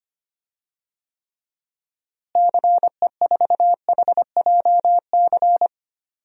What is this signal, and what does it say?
Morse code, heard as a keyed tone, about 25 wpm, 700 Hz. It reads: CE4HJC
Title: DXCC-100 Call Signs - 6 Characters - Repeated 2x - 25wpm